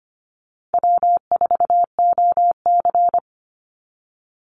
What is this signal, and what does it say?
Morse code, heard as a keyed tone, about 25 wpm, 700 Hz. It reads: W4OC
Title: Easy Call Signs Repeated 2x - 25wpm